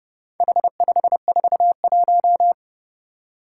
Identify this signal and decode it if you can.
Morse code, heard as a keyed tone, about 31 wpm, 700 Hz. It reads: H541
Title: Four Challenging Letter-Number Combinations 30wpm